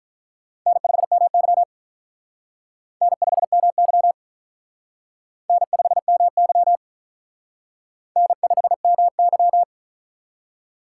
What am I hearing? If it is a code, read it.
Morse code, heard as a keyed tone, about 42 wpm, 700 Hz. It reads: N5MY N5MY N5MY N5MY
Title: Speed Racing - USA Call Signs 35wpm